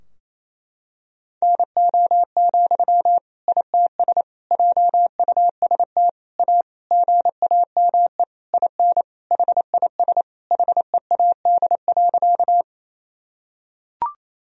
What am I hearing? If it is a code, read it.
Morse code, heard as a keyed tone, about 28 wpm, 700 Hz. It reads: NO, ITS JUST A GAME IN HIS HEAD.
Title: Sentences from Top 1000 Words - Review Part 3 - 28wpm